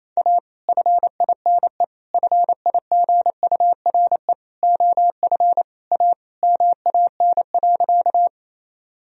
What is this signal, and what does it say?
Morse code, heard as a keyed tone, about 28 wpm, 700 Hz. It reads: A FINE FIGURE OF A MAN.
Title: Sentences from Top 1000 Words - Review Part 2 - 28wpm